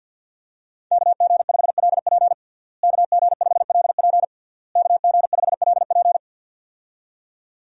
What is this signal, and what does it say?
Morse code, heard as a keyed tone, about 51 wpm, 700 Hz. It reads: KG5LP KG5LP KG5LP
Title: USA Call Signs - 5 Characters - Repeated 3x - 50wpm